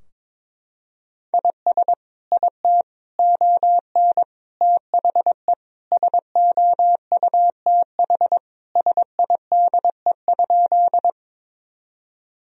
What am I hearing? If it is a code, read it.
Morse code, heard as a keyed tone, about 22 wpm, 700 Hz. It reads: IS IT ON THE SOUTH SIDE?
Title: Sentences from Top 1000 Words - Review Part 2 - 22wpm